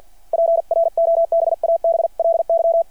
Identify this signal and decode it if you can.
Morse code, heard as a keyed tone, about 53 wpm, spaced as for 46 wpm, 690 Hz. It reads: PROBABLY
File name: archived / CRSnet